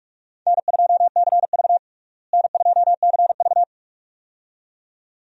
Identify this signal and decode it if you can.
Morse code, heard as a keyed tone, about 46 wpm, 700 Hz. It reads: N2CV N2CV
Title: USA Call Signs - 4 Characters - Repeated 2x - 45wpm